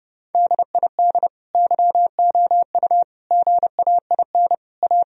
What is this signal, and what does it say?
Morse code, heard as a keyed tone, about 30 wpm, 700 Hz. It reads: DID YOU GAIN A
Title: Sentences from Top 1000 Words - Review Part 2 - 30wpm